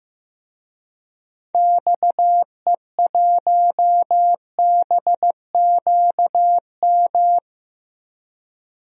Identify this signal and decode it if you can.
Morse code, heard as a keyed tone, about 15 wpm, 700 Hz. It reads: XE1BQM
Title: DXCC-100 Call Signs - 6 Characters - Repeated 2x - 15wpm